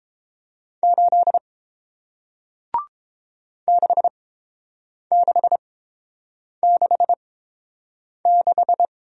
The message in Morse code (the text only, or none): 8 6666